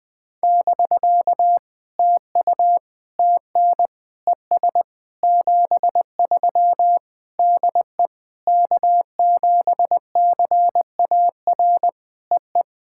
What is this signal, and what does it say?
Morse code, heard as a keyed tone, about 21 wpm, 700 Hz. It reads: <BK> TU TN ES 73 DE K7CAR EE
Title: POTA QSO Practice - 20wpm